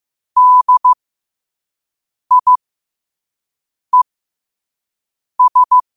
Morse code, heard as a keyed tone, about 15 wpm, spaced as for 5 wpm, 1000 Hz. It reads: DIES